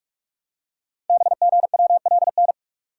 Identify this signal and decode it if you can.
Morse code, heard as a keyed tone, about 44 wpm, 700 Hz. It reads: BGWLN
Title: Code Groups - 5 Letters - 45wpm